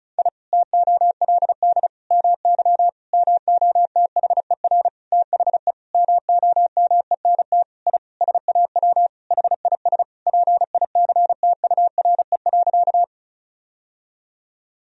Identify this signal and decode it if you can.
Morse code, heard as a keyed tone, about 34 wpm, 700 Hz. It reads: I TOLD MY MOTHER THE MOMENT I SAW HIS PICTURE.
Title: Sentences from Top 1000 Words - Review Part 1 - 35wpm